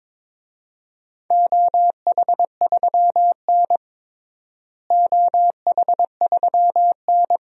Morse code, heard as a keyed tone, about 22 wpm, 700 Hz. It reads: OH3N OH3N
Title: DXCC-100 Call Signs - 4 Characters - Repeated 3x - 22wpm